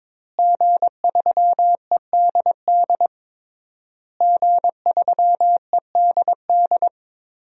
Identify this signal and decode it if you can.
Morse code, heard as a keyed tone, about 21 wpm, 700 Hz. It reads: G3EDD G3EDD
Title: DXCC-100 Call Signs - 5 Characters - Repeated 3x - 22wpm